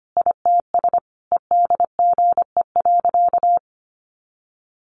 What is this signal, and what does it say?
Morse code, heard as a keyed tone, about 25 wpm, 700 Hz. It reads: ITS EDGE.
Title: Sentences from Top 1300 Words 25wpm